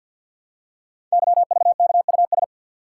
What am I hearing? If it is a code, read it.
Morse code, heard as a keyed tone, about 52 wpm, 700 Hz. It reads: YVKUS
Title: Code Groups - 5 Letters - 50wpm